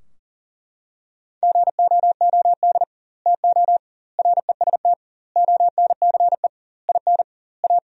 Morse code, heard as a keyed tone, about 40 wpm, 700 Hz. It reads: GOOD TO REST ONCE IN A
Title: 2x Word-Spacing - Sentences from Top 500 Words - 40wpm